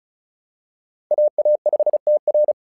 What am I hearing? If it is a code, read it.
Morse code, heard as a keyed tone, about 36 wpm, 600 Hz. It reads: AA5TR
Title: SOTA QSO Practice 35wpm